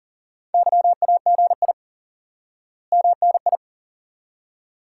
Morse code, heard as a keyed tone, about 40 wpm, 700 Hz. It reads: YAGI MNI
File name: course / instant-qso-element-course-3 / IQE Course 3 - Lesson 89 - YAGI - Two Elements - 40wpm